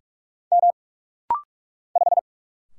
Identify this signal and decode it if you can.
Morse code, heard as a keyed tone, about 46 wpm, 700 Hz. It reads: M 5
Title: Single Letter-Number - Rapid-Fire 45wpm